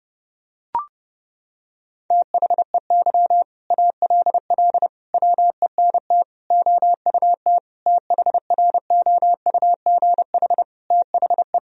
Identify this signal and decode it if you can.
Morse code, heard as a keyed tone, about 30 wpm, 700 Hz. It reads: THEY ALL WENT OUT THROUGH THE
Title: Sentences from Top 1000 Words - Review Part 1 - 30wpm